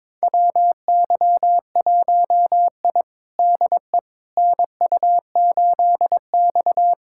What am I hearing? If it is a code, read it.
Morse code, heard as a keyed tone, about 22 wpm, 700 Hz. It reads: WY1I DE NU8X
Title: USA Callsign DE Callsign - 4 Characters - 22wpm